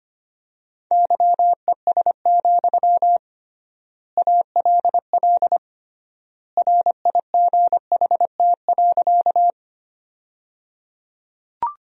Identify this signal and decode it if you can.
Morse code, heard as a keyed tone, about 25 wpm, 700 Hz. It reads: YES, ALL RIGHT.
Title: 3x Word-Spacing - Sentences from Top 500 Words - 25wpm